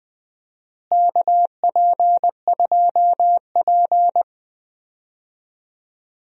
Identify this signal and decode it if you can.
Morse code, heard as a keyed tone, about 20 wpm, 700 Hz. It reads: KP2P
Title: DXCC-100 Call Signs - 4 Characters - Repeated 2x - 20wpm